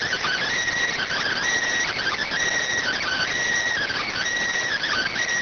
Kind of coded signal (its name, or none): Emergency Alert System (SAME) tones